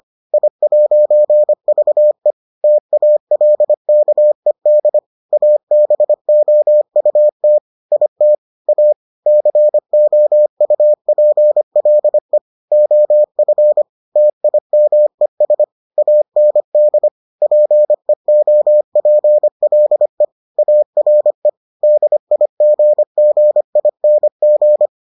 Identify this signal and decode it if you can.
Morse code, heard as a keyed tone, about 25 wpm, 600 Hz. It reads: I'VE TALKED ABOUT IT A COUPLE OF TIMES AND PEOPLE ARE DIGGING